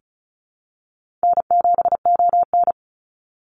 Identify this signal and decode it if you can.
Morse code, heard as a keyed tone, about 34 wpm, 700 Hz. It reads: N7ON
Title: Call Signs Repeated 3x - 35wpm